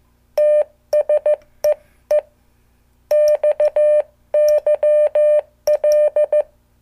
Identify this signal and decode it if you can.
Morse code, heard as a keyed tone, about 15 wpm, 600 Hz. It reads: TS EE XYL